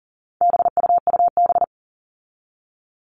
Morse code, heard as a keyed tone, about 40 wpm, 700 Hz. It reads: BUUB
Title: All Cut Numbers - 4-Digit 40wpm